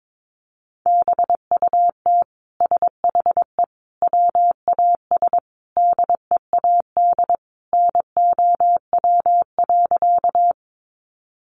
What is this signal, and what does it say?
Morse code, heard as a keyed tone, about 22 wpm, 700 Hz. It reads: BUT SHE WAS DEAD NOW.
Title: Sentences from Top 1500 Words 22wpm